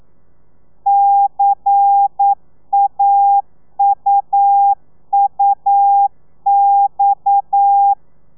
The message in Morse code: CAUUX